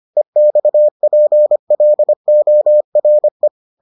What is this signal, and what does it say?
Morse code, heard as a keyed tone, about 25 wpm, 600 Hz. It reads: EXPLORE